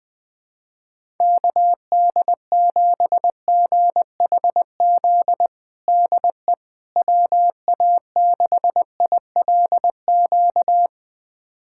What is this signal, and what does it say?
Morse code, heard as a keyed tone, about 21 wpm, 700 Hz. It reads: KD7GHZ DE WA6ILQ
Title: USA Callsign DE Callsign - 6 Characters - 20wpm